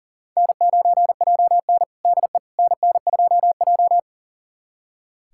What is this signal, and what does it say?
Morse code, heard as a keyed tone, about 37 wpm, 700 Hz. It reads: N9JN DE NN2J
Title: USA Callsign DE Callsign - 4 Characters - 40wpm